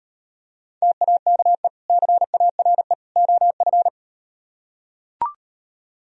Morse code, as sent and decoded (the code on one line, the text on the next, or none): - .- -.- . / -.-. .- .-. / --- ..-.
TAKE CARE OF